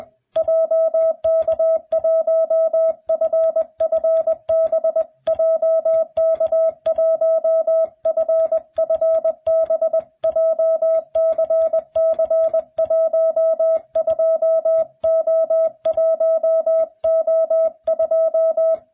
Morse code, heard as a keyed tone, about 21 wpm, 660 Hz. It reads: JK1FFB JK1FFB JCC12O1O2